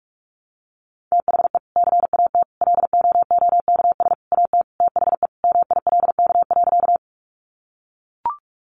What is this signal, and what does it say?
Morse code, heard as a keyed tone, about 44 wpm, 700 Hz. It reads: THE CAT LOOKS AT THE MILK.
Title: Additional Easy Phrases 45wpm